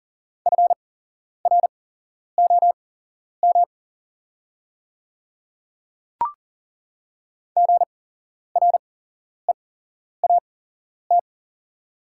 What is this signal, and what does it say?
Morse code, heard as a keyed tone, about 38 wpm, spaced as for 11 wpm, 700 Hz. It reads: FROM GREAT
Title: 8x Character Spacing - Top 200 Words - 40wpm